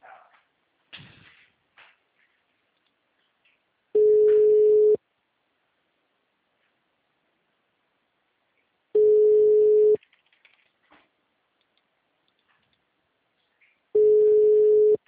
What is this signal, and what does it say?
Ringback tone (Europe, 425 Hz).
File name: out- -2012- - - .27577